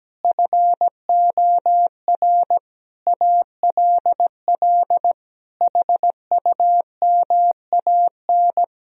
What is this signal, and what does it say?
Morse code, heard as a keyed tone, about 17 wpm, 700 Hz. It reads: FOR ALL HUMAN
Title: Sentences from Top 1000 Words - Review Part 4 - 17wpm